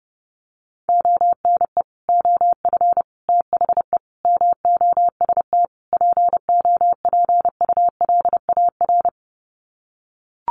Morse code, heard as a keyed tone, about 30 wpm, 700 Hz. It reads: ONE OF THE MOST POPULAR
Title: Sets of 5 Words 30wpm